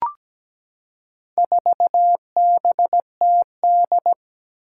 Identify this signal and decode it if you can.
Morse code, heard as a keyed tone, about 17 wpm, 700 Hz. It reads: 4BTD
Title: Four Challenging Letter-Number Combinations 17wpm